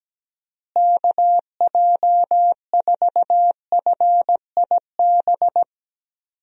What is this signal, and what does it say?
Morse code, heard as a keyed tone, about 17 wpm, 700 Hz. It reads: KJ4FIB
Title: USA Call Signs - 6 Characters - Repeated 3x - 17wpm